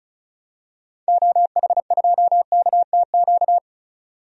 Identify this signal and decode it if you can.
Morse code, heard as a keyed tone, about 34 wpm, 700 Hz. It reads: OH2KTQ
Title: DXCC-100 Call Signs - 6 Characters - 35wpm